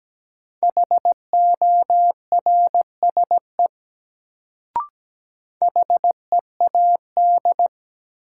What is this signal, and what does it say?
Morse code, heard as a keyed tone, about 17 wpm, 700 Hz. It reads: HORSE HEAD
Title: Sentences from Top 300 Words - New Word Drill - 17wpm